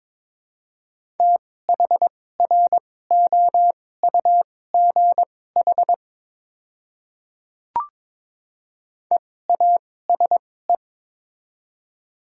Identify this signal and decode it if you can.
Morse code, heard as a keyed tone, about 22 wpm, spaced as for 16 wpm, 700 Hz. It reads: THROUGH EASE
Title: 2x Character Spacing - Top 300 Words - 22wpm